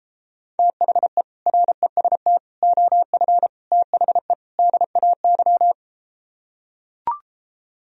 Morse code, heard as a keyed tone, about 33 wpm, 700 Hz. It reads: THE REST OF THE DAY